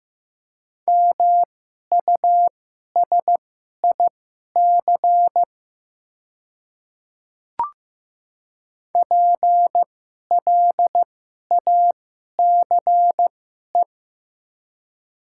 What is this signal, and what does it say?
Morse code, heard as a keyed tone, about 15 wpm, spaced as for 11 wpm, 700 Hz. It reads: MUSIC PLACE
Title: 2x Character Spacing - Top 300 Words - 15wpm